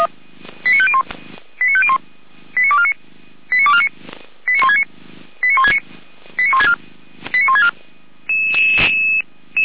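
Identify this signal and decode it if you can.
Five-tone selective call (ZVEI), heard as quick runs of five tones, tones of about 70 ms. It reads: 89501 89501 89249 89249 89169 89169 89164 89164
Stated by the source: ZVEI five-tone call received on SDR.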